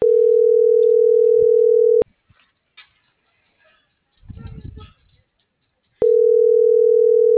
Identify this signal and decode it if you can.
Ringback tone (North America), 440 + 480 Hz.